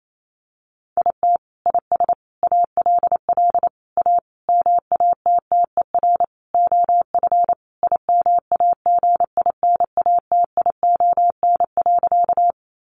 Morse code, heard as a keyed tone, about 28 wpm, 700 Hz. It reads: IT IS ALL A MATTER OF IMAGINATION.
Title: Sentences from Top 1300 Words 28wpm